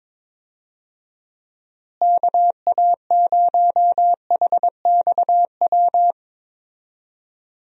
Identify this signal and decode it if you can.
Morse code, heard as a keyed tone, about 22 wpm, 700 Hz. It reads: KA0HXW
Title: USA Call Signs - 6 Characters - Repeated 2x - 22wpm